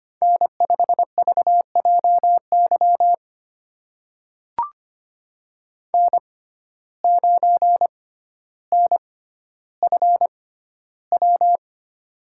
Morse code, heard as a keyed tone, about 25 wpm, spaced as for 13 wpm, 700 Hz. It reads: N5VJY N9NFW
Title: USA Call Signs - 5 Characters - 6x Character Spacing - 25wpm